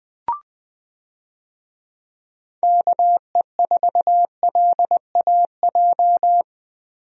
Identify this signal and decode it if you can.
Morse code, heard as a keyed tone, about 21 wpm, 700 Hz. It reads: KE4LAJ
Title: USA Call Signs - 6 Characters - Repeated 3x - 20wpm